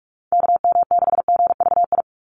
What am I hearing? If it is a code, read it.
Morse code, heard as a keyed tone, about 44 wpm, 700 Hz. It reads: KM6GVI
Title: USA Call Signs 45wpm